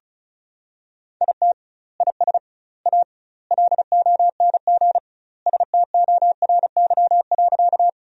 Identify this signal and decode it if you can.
Morse code, heard as a keyed tone, about 36 wpm, 700 Hz. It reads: IT IS A LONG STORY.
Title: 2x Word-Spacing - Sentences from Top 300 Words - 35wpm